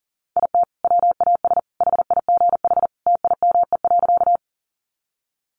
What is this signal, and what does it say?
Morse code, heard as a keyed tone, about 40 wpm, 700 Hz. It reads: IT WAS HIGH TIME.
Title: Sentences from Top 200 Words 40wpm